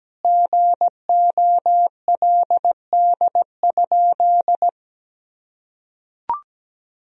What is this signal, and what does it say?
Morse code, heard as a keyed tone, about 17 wpm, 700 Hz. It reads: GOLD?
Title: Sentences from Top 1000 Words - Review Part 3 - 17wpm